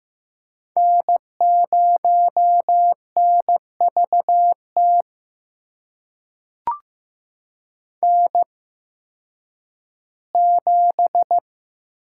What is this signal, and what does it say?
Morse code, heard as a keyed tone, about 15 wpm, spaced as for 10 wpm, 700 Hz. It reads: N0NVT N7
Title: USA Call Signs - 5 Characters - 8x Character Spacing - 15wpm